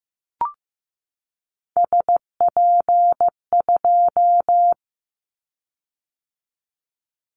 Morse code, heard as a keyed tone, about 15 wpm, 700 Hz. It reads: SP2
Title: Callsign Prefixes for Top 25 DXCC - 15wpm